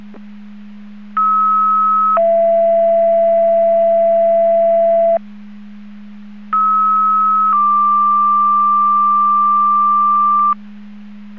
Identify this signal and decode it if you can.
Two-tone page, heard as a one-second tone, then a three-second tone, about 1290 then 670 Hz, about 1290 then 1180 Hz. It is heard from 1.2 to 10.5 s.